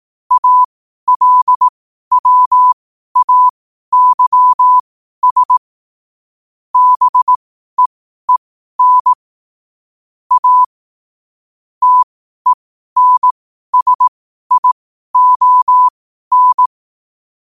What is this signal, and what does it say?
Morse code, heard as a keyed tone, about 18 wpm, spaced as for 12 wpm, 1000 Hz. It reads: ALWAYS BEEN A TENSION
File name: qotdfiles / 12wpm